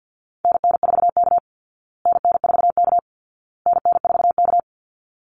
Morse code, heard as a keyed tone, about 50 wpm, 700 Hz. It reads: NN4K NN4K NN4K
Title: Call Signs Repeated 2x - 50wpm